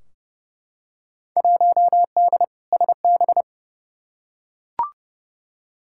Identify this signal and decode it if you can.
Morse code, heard as a keyed tone, about 32 wpm, 700 Hz. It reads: 1D SB
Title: Field Day Exchanges - 30wpm